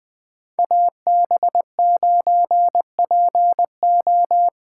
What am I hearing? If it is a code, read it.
Morse code, heard as a keyed tone, about 20 wpm, 700 Hz. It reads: AB9PO
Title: Speed Racing - USA 2x2 Call Signs 17wpm